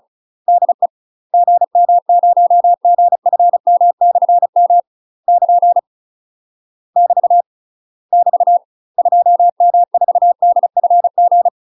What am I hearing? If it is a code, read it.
Morse code, heard as a keyed tone, about 35 wpm, 700 Hz. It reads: DE GM0GFM/M <KN> <BT> <BT> 2M4DFG
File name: UK MASTER QSOs TRAINING / 35WPM / UK MASTER QSOs 0005